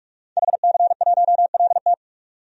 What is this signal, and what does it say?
Morse code, heard as a keyed tone, about 41 wpm, 700 Hz. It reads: HC1LT
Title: DXCC-100 Call Signs - 5 Characters - 45wpm